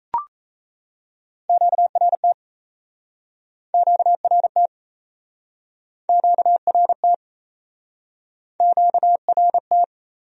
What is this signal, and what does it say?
Morse code, heard as a keyed tone, about 40 wpm, 700 Hz. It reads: QRT QRT QRT QRT